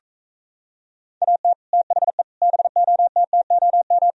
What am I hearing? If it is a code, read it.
Morse code, heard as a keyed tone, about 45 wpm, 700 Hz. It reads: AT THE BOTTOM